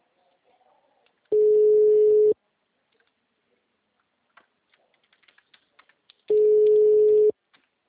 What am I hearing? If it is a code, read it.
Ringback tone (Europe, 425 Hz).